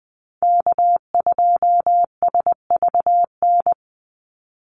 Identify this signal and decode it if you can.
Morse code, heard as a keyed tone, about 20 wpm, 700 Hz. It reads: K2SVN
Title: USA Call Signs 20wpm